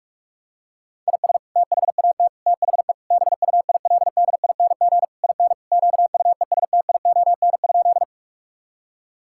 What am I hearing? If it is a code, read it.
Morse code, heard as a keyed tone, about 46 wpm, 700 Hz. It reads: IS THAT THE BUILDING IN QUESTION?
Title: Sentences from Top 1000 Words - Review Part 5 - 45wpm